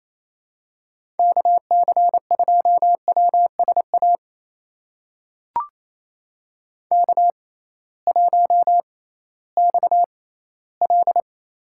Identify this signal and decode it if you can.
Morse code, heard as a keyed tone, about 28 wpm, spaced as for 16 wpm, 700 Hz. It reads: KC2WSA K1XL
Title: USA Call Signs - 6x Character Spacing - 28wpm